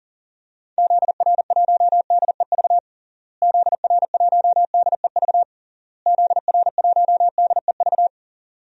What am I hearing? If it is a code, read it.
Morse code, heard as a keyed tone, about 40 wpm, 700 Hz. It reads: ZR1DEV ZR1DEV ZR1DEV
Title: DXCC-100 Call Signs - 6 Characters - Repeated 3x - 40wpm